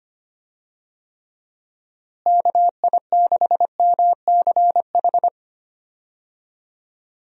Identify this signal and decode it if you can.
Morse code, heard as a keyed tone, about 25 wpm, 700 Hz. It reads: KI6MCH
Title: USA Call Signs - 6 Characters - Repeated 2x - 25wpm